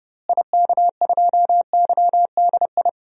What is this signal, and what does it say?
Morse code, heard as a keyed tone, about 32 wpm, 700 Hz. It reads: IK2YDI
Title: DXCC-100 Call Signs - 6 Characters - Repeated 3x - 30wpm